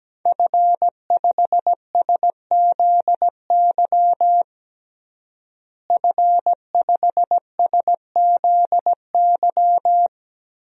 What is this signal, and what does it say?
Morse code, heard as a keyed tone, about 17 wpm, 700 Hz. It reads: F5SZY F5SZY
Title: DXCC-100 Call Signs - 5 Characters - Repeated 3x - 17wpm